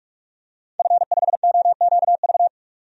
Morse code, heard as a keyed tone, about 41 wpm, 700 Hz. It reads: F5OQV
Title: DXCC-100 Call Signs - 5 Characters - 45wpm